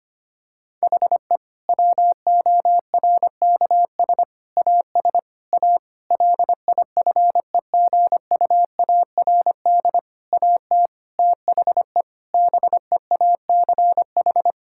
Morse code, heard as a keyed tone, about 25 wpm, 700 Hz. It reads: HE WORKS AS A LIFEGUARD AT THE BEACH
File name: practice / 25wpm / Difficult Sentences with Hints - 25wpm